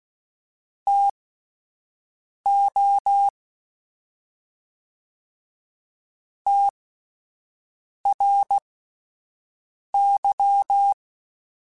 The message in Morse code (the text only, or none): TO TRY